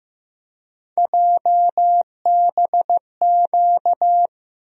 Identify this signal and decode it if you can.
Morse code, heard as a keyed tone, about 15 wpm, 700 Hz. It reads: JBQ
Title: USA Call Sign Postfixes - 3 Letters 15wpm